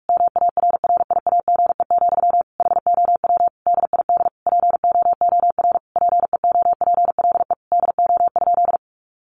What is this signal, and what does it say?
Morse code, heard as a keyed tone, about 44 wpm, 700 Hz. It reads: MARRIAGE, HOW DID POOR PEOPLE DO?
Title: Sentences from Top 1100 Words 45wpm